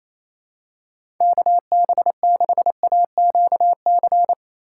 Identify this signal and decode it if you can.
Morse code, heard as a keyed tone, about 28 wpm, 700 Hz. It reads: KB6AQC
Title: USA Call Signs - 6 Characters - 6x Character Spacing - 28wpm